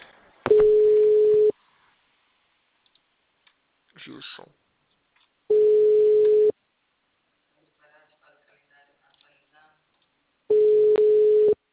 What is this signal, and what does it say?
Ringback tone (Europe, 425 Hz), with speech.
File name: out- -2014- - - .15594